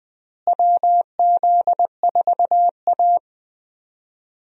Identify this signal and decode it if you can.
Morse code, heard as a keyed tone, about 20 wpm, 700 Hz. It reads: WZ4A
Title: USA Call Signs - 4 Characters - Repeated 3x - 20wpm